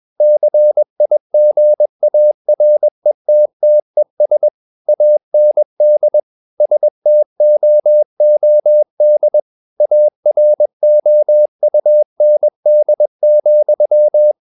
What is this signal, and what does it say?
Morse code, heard as a keyed tone, about 21 wpm, 600 Hz. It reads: CIGARETTES AND STOOD AROUND,
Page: CW med Gnister